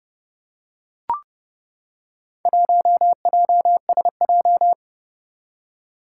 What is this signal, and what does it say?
Morse code, heard as a keyed tone, about 29 wpm, 690 Hz. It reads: 1JSJ
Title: Four Challenging Letter-Number Combinations 30wpm